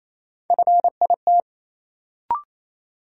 Morse code, heard as a keyed tone, about 28 wpm, 700 Hz. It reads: FIT
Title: Sentences from Top 800 Words - New Word Drill - 28wpm